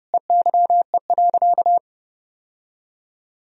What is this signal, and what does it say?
Morse code, heard as a keyed tone, about 32 wpm, 700 Hz. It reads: EYE.
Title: Sentences from Top 1000 Words - Review Part 1 - 30wpm